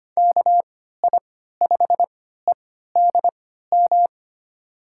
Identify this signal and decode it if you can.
Morse code, heard as a keyed tone, about 25 wpm, spaced as for 14 wpm, 700 Hz. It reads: KI5EDM
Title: USA Call Signs - 6 Characters - 3x Character Spacing - 25wpm